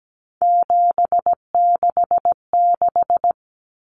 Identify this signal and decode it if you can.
Morse code, heard as a keyed tone, about 17 wpm, 700 Hz. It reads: 766
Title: Numbers - 3-Digit 17wpm